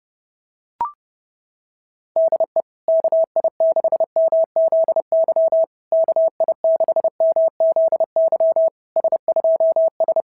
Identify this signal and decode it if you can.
Morse code, heard as a keyed tone, about 26 wpm, 660 Hz. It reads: DE KI6MZY KI6MZY S2S
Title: SOTA QSO Practice 30wpm